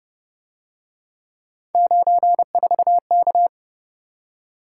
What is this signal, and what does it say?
Morse code, heard as a keyed tone, about 30 wpm, 700 Hz. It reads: 94K